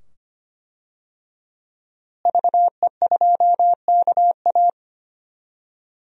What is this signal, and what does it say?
Morse code, heard as a keyed tone, about 25 wpm, 700 Hz. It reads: VE2KA
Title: DXCC-100 Call Signs - 5 Characters - Repeated 2x - 25wpm